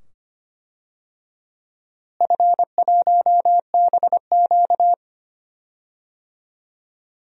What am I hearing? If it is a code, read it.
Morse code, heard as a keyed tone, about 25 wpm, 700 Hz. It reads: F1BQ
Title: DXCC-100 Call Signs - 4 Characters - Repeated 2x - 25wpm